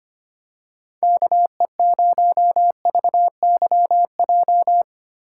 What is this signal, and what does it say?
Morse code, heard as a keyed tone, about 25 wpm, 700 Hz. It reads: KE0VYJ
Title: USA Call Signs - 6 Characters - 2x Character Spacing - 25wpm